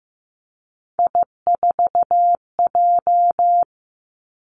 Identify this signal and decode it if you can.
Morse code, heard as a keyed tone, about 15 wpm, 700 Hz. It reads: I4J
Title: Three Challenging Letter-Number Combinations 15wpm